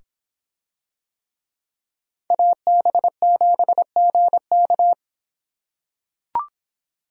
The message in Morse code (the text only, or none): AB7GK